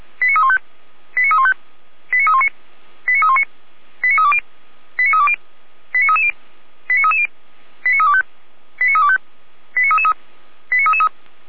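Five-tone selective call (ZVEI), heard as quick runs of five tones, tones of about 70 ms. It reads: 89315 89315 89319 89319 89320 89320 89330 89330 89325 89325 89333 89333
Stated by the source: ZVEI five-tone selective call 89337, received by SDR, on the Feuerwehr Orlamünde audio server.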